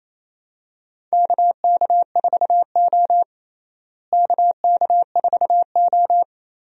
Morse code, heard as a keyed tone, about 28 wpm, 700 Hz. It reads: KK4O KK4O
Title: USA Call Signs - 4 Characters - Repeated 2x - 28wpm